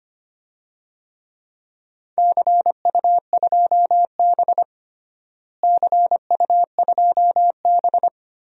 Morse code, heard as a keyed tone, about 25 wpm, 700 Hz. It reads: CU2B CU2B
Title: DXCC-100 Call Signs - 4 Characters - Repeated 3x - 25wpm